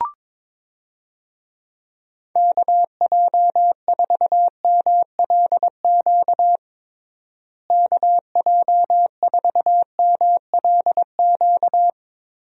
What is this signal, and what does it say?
Morse code, heard as a keyed tone, about 22 wpm, 700 Hz. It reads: KJ4MLQ KJ4MLQ
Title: USA Call Signs - 6 Characters - Repeated 3x - 22wpm